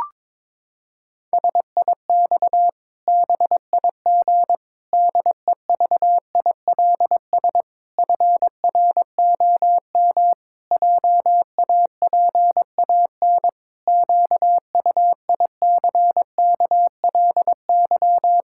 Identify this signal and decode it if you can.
Morse code, heard as a keyed tone, about 22 wpm, 700 Hz. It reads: SIX BIG DEVILS FROM JAPAN QUICKLY
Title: Pangram Challenge v2 - 22wpm